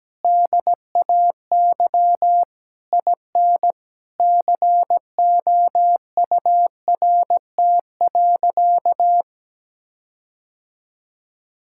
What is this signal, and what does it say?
Morse code, heard as a keyed tone, about 17 wpm, 700 Hz. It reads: DAY IN COURT.
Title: Sentences from Top 1600 Words 17wpm